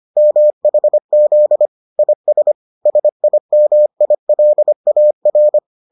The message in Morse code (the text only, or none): MHZ IS SIMILAR